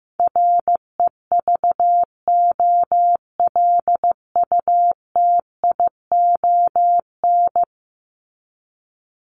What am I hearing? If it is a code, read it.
Morse code, heard as a keyed tone, about 15 wpm, 700 Hz. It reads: REVOLUTION
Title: Top 1751-2000 Words 15wpm